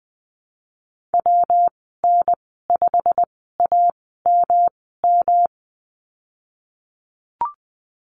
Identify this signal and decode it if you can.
Morse code, heard as a keyed tone, about 20 wpm, spaced as for 15 wpm, 700 Hz. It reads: WN5AMM
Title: 140 Lesson 23g - Letter f - Call Signs 20wpm